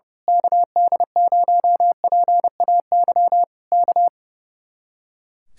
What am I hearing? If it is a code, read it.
Morse code, heard as a keyed tone, about 30 wpm, 700 Hz. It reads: KD0PAY K
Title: Speed-Racing Common QSO Phrases - 30wpm